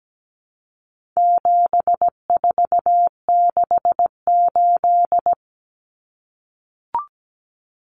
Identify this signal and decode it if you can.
Morse code, heard as a keyed tone, about 17 wpm, 700 Hz. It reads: 7468
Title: Numbers - 4-Digit 17wpm